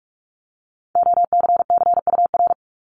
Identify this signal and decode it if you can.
Morse code, heard as a keyed tone, about 46 wpm, 690 Hz. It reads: OCCUR
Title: Multi-letter Phonemes 45wpm